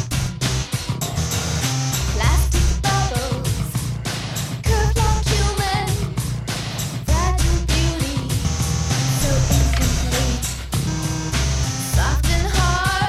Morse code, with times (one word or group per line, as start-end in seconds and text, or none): none